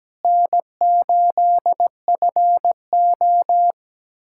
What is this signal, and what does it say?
Morse code, heard as a keyed tone, about 17 wpm, 700 Hz. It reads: N8FO
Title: USA Callsign DE Callsign - 4 Characters - 17wpm